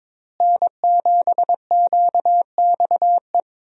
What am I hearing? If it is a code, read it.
Morse code, heard as a keyed tone, about 22 wpm, 700 Hz. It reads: N7QXE
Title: USA Call Signs - 5 Characters - Repeated 2x - 22wpm